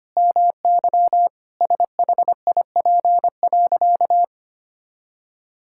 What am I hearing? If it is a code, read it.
Morse code, heard as a keyed tone, about 25 wpm, 700 Hz. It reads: MY SHIP.
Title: Sentences from Top 1000 Words - Review Part 5 - 25wpm